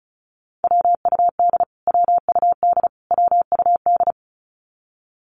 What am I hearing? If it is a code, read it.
Morse code, heard as a keyed tone, about 34 wpm, 700 Hz. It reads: WUD WUD WUD
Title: Abbreviations Repeated 3x - 35wpm